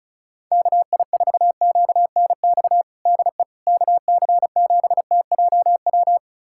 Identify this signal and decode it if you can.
Morse code, heard as a keyed tone, about 36 wpm, 700 Hz. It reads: KI4QNX DE KC7TJW
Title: USA Callsign DE Callsign - 6 Characters - 35wpm